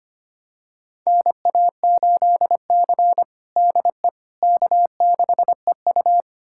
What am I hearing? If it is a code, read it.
Morse code, heard as a keyed tone, about 25 wpm, 700 Hz. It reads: NA8C DE K6EU
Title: USA Callsign DE Callsign - 4 Characters - 25wpm